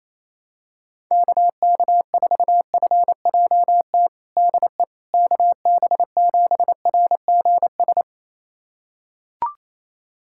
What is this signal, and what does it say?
Morse code, heard as a keyed tone, about 28 wpm, 700 Hz. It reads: KK4FJT DE KB7RGS
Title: USA Callsign DE Callsign - 6 Characters - 28wpm